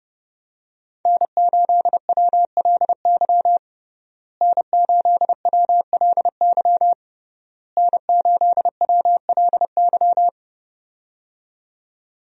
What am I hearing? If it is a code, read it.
Morse code, heard as a keyed tone, about 30 wpm, 700 Hz. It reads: N8WLY N8WLY N8WLY
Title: USA Call Signs - 5 Characters - Repeated 3x - 30wpm